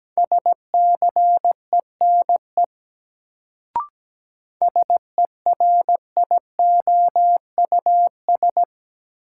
Morse code, heard as a keyed tone, about 17 wpm, 700 Hz. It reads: SCENE SERIOUS
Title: Sentences from Top 1100 Words - New Word Drill - 17wpm